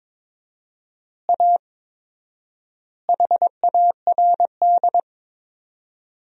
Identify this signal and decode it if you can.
Morse code, heard as a keyed tone, about 22 wpm, 700 Hz. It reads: A HARD
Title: 4x Word-Spacing - Sentences from Top 300 Words - 22wpm